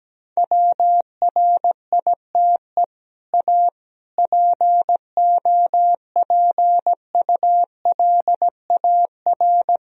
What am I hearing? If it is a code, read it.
Morse code, heard as a keyed tone, about 17 wpm, 700 Hz. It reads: WRITE A POPULAR
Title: Sentences from Top 1000 Words - Review Part 3 - 17wpm